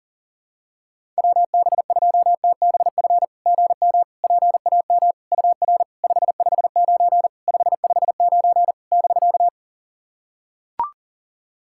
Morse code, heard as a keyed tone, about 40 wpm, 690 Hz. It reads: WB2TBF GM PAM UR 559 559 <BK>
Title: SOTA QSO Practice 40wpm